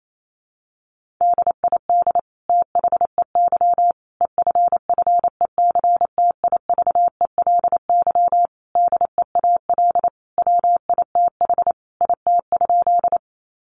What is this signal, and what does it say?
Morse code, heard as a keyed tone, about 28 wpm, 700 Hz. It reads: DID THEY EFFECTIVELY DEAL WITH IT?
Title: Sentences from Top 1400 Words 28wpm